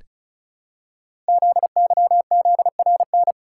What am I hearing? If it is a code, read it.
Morse code, heard as a keyed tone, about 36 wpm, 700 Hz. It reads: ZYZRN
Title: Code Groups - 5 Letters - 35wpm